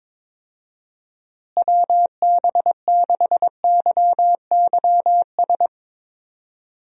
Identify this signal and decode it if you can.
Morse code, heard as a keyed tone, about 21 wpm, 700 Hz. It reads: WB6YYS
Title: USA Call Signs - 6 Characters - Repeated 2x - 22wpm